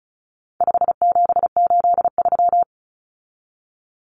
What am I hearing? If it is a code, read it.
Morse code, heard as a keyed tone, about 34 wpm, 700 Hz. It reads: 5783
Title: Numbers - 4-Digit 35wpm